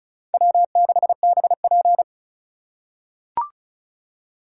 Morse code, heard as a keyed tone, about 36 wpm, 700 Hz. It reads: W6BP
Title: USA Call Signs - 3x Character Spacing - 35wpm